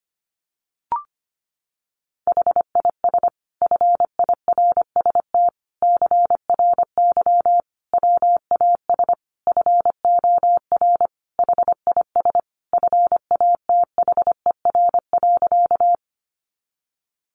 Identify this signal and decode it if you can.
Morse code, heard as a keyed tone, about 25 wpm, 700 Hz. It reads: HIS FIRST CRY WAS FOR HIS FATHER.
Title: Sentences from Top 500 Words 25wpm